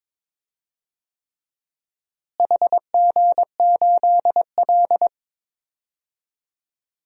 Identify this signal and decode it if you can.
Morse code, heard as a keyed tone, about 21 wpm, 700 Hz. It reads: HG8L
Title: DXCC-100 Call Signs - 4 Characters - Repeated 3x - 22wpm